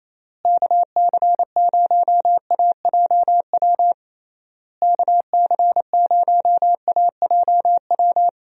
Morse code, heard as a keyed tone, about 28 wpm, 700 Hz. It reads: KC0AJW KC0AJW
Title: USA Call Signs - 6 Characters - Repeated 3x - 28wpm